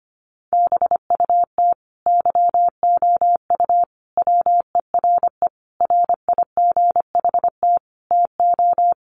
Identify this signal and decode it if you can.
Morse code, heard as a keyed tone, about 25 wpm, 700 Hz. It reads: BUT YOU WERE RIGHT TO
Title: Sentences from Top 800 Words 25wpm